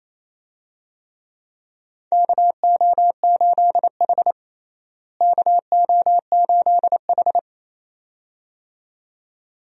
Morse code, heard as a keyed tone, about 28 wpm, 700 Hz. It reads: KO8H KO8H
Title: USA Call Signs - 4 Characters - Repeated 2x - 28wpm